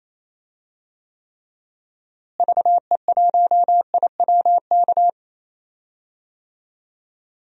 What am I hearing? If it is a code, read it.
Morse code, heard as a keyed tone, about 27 wpm, 700 Hz. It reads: VE1IWK
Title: DXCC-100 Call Signs - 6 Characters - Repeated 2x - 28wpm